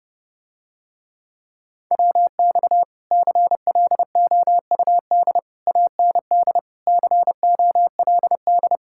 Morse code, heard as a keyed tone, about 30 wpm, 700 Hz. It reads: WX CLOUD AND COLD